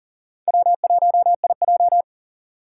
Morse code, heard as a keyed tone, about 40 wpm, 700 Hz. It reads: W1IJ
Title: USA Call Signs - 2x Character Spacing - 40wpm